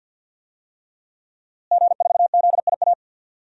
Morse code, heard as a keyed tone, about 53 wpm, 700 Hz. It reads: G4ZIA
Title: DXCC-100 Call Signs - 5 Characters - Repeated 3x - 50wpm